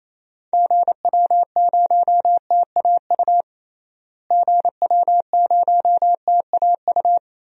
Morse code, heard as a keyed tone, about 28 wpm, 700 Hz. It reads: GW0TAU GW0TAU
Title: DXCC-100 Call Signs - 6 Characters - Repeated 3x - 28wpm